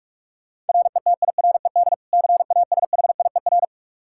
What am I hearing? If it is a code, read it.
Morse code, heard as a keyed tone, about 47 wpm, 700 Hz. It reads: RETIRED CASHIER
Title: Practice QSO - Exchange 3 - More Info - 45wpm